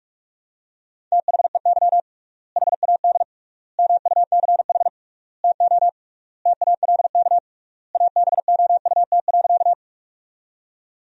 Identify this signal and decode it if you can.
Morse code, heard as a keyed tone, about 41 wpm, 700 Hz. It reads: THEY HAD MUCH TO TALK ABOUT.
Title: 3x Word-Spacing - Sentences from Top 400 Words - 45wpm